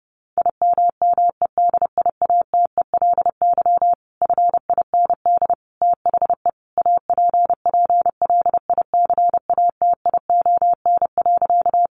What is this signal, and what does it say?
Morse code, heard as a keyed tone, about 30 wpm, 700 Hz. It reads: IMMEDIATELY FIND THE APPLICATION.
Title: Sentences from Top 1200 Words 30wpm